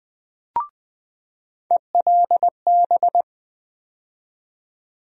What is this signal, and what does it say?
Morse code, heard as a keyed tone, about 20 wpm, 700 Hz. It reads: ELB
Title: USA Call Sign Postfixes - 3 Letters 20wpm